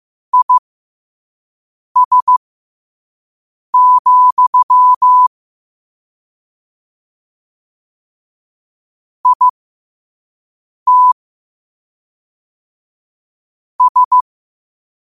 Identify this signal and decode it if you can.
Morse code, heard as a keyed tone, about 15 wpm, spaced as for 5 wpm, 1000 Hz. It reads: IS, IT S
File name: qotdfiles / 5wpm